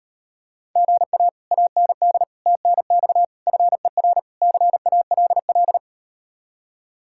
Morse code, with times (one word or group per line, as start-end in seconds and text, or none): 0.8-1.3 GA
1.5-2.2 AND
2.5-3.2 TNX
3.5-4.2 FER
4.4-5.8 CALL